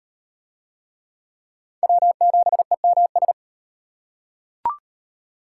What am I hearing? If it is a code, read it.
Morse code, heard as a keyed tone, about 34 wpm, 700 Hz. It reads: W7EMS